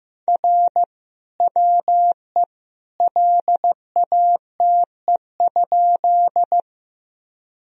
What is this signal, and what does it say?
Morse code, heard as a keyed tone, about 15 wpm, 700 Hz. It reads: R WE LATE?
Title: Sentences from Top 300 Words and Abbreviations 15wpm